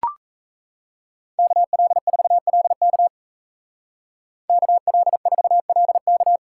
Morse code, heard as a keyed tone, about 35 wpm, 700 Hz. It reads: KL4LK KL4LK
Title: Speed Racing - USA Call Signs 28wpm